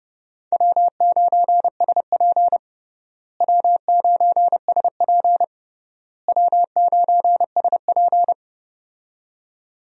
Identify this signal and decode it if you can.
Morse code, heard as a keyed tone, about 32 wpm, 700 Hz. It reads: W9SP W9SP W9SP
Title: USA Call Signs - 4 Characters - Repeated 3x - 30wpm